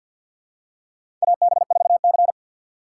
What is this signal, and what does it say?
Morse code, heard as a keyed tone, about 48 wpm, 700 Hz. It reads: AB4C
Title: USA Call Signs - 4 Characters - 3x Character Spacing - 50wpm